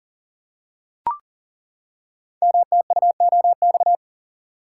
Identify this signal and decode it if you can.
Morse code, heard as a keyed tone, about 40 wpm, 700 Hz. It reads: MTUOX
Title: Code Groups - 5 Letters - 40wpm